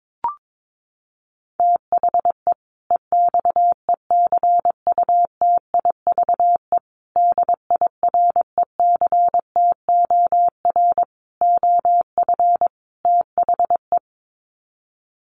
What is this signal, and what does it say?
Morse code, heard as a keyed tone, about 22 wpm, 700 Hz. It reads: THE EXECUTIVE DIRECTOR OF THE
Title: Sets of 5 Words Encore 22wpm